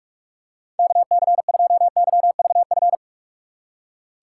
Morse code, heard as a keyed tone, about 46 wpm, 690 Hz. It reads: KC2YVF
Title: USA Call Signs - 4x Character Spacing - 45wpm